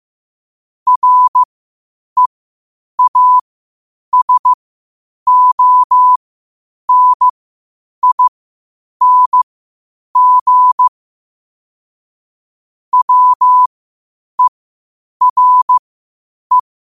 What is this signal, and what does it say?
Morse code, heard as a keyed tone, about 15 wpm, spaced as for 8 wpm, 1000 Hz. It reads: REASONING WERE
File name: qotdfiles / 8wpm